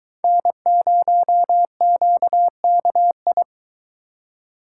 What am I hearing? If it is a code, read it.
Morse code, heard as a keyed tone, about 23 wpm, 700 Hz. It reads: N0QKI